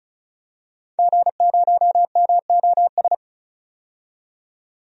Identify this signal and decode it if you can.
Morse code, heard as a keyed tone, about 36 wpm, 700 Hz. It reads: G0MOS
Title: DXCC-100 Call Signs - 5 Characters - 35wpm